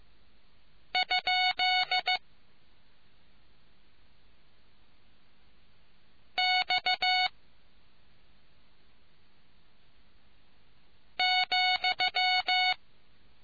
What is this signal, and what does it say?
Morse code, heard as a keyed tone, about 15 wpm, 740 Hz. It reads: ? X ,